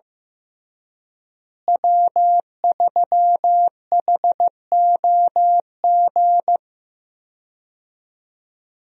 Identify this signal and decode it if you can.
Morse code, heard as a keyed tone, about 16 wpm, 700 Hz. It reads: W3HOG
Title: USA Call Signs - 5 Characters - Repeated 3x - 15wpm